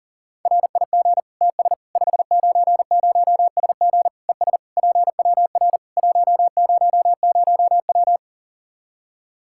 Morse code, heard as a keyed tone, about 37 wpm, 700 Hz. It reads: RIG TS 590SG ES PWR 100W
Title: Practice QSO - Exchange 2 - Info - 40wpm